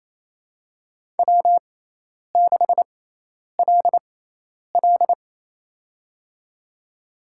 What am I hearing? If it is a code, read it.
Morse code, heard as a keyed tone, about 28 wpm, spaced as for 10 wpm, 700 Hz. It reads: W6LL
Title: USA Call Signs - 6x Character Spacing - 28wpm